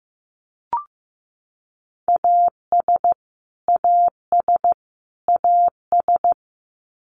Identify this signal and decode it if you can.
Morse code, heard as a keyed tone, about 15 wpm, 700 Hz. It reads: AS AS AS
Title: Top 300 Words Repeated 15wpm